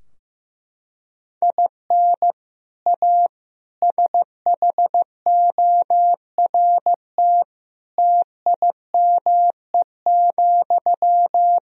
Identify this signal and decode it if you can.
Morse code, heard as a keyed tone, about 16 wpm, 700 Hz. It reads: IN A SHORT TIME,
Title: Sentences from Top 1000 Words - Review Part 2 - 15wpm